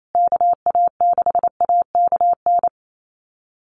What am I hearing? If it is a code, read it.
Morse code, heard as a keyed tone, about 28 wpm, 700 Hz. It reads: KA6AKN
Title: USA Call Signs Repeated 3x 28wpm